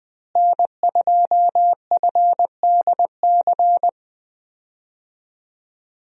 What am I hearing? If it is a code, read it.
Morse code, heard as a keyed tone, about 20 wpm, 700 Hz. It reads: N2FDC
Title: USA Call Signs - 5 Characters - Repeated 2x - 20wpm